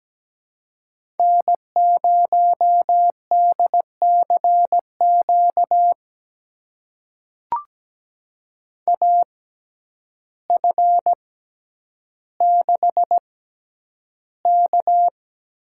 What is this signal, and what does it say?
Morse code, heard as a keyed tone, about 17 wpm, spaced as for 9 wpm, 700 Hz. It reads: N0DCQ AF6K
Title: USA Call Signs - 6x Character Spacing - 17wpm